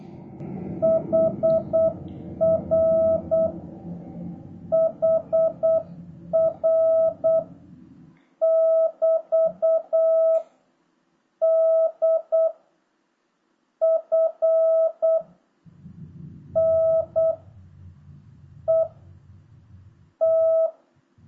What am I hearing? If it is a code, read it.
Morse code, heard as a keyed tone, about 8 wpm, 650 Hz. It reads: HR HR <BT> D F N E T